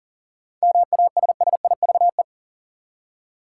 Morse code, heard as a keyed tone, about 40 wpm, 700 Hz. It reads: MASSIVE
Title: Seven Letter Words - 40wpm